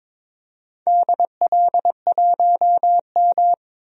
Morse code, heard as a keyed tone, about 22 wpm, 700 Hz. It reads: DL1M
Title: DXCC-100 Call Signs - 4 Characters - 22wpm